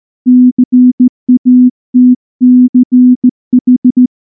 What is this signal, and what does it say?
Morse code, heard as a keyed tone, about 14 wpm, 250 Hz. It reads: CATCH
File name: morse / morse-training-15WPM